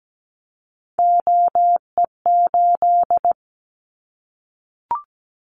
Morse code, heard as a keyed tone, about 17 wpm, 700 Hz. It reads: OE8
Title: Callsign Prefixes for Top 100 DXCC - 17wpm